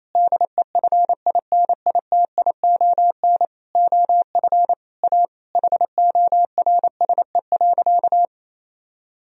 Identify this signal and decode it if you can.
Morse code, heard as a keyed tone, about 28 wpm, 700 Hz. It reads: DEFINITION OF A HORSE.
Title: Sentences from Top 1000 Words - Review Part 4 - 28wpm